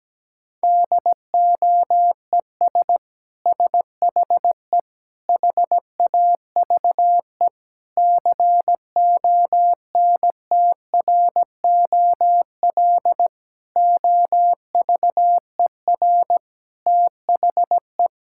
Morse code, heard as a keyed tone, about 17 wpm, 700 Hz. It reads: DOES SHE HAVE CONTROL OVER THE
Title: Sentences from Top 1000 Words - Review Part 5 - 17wpm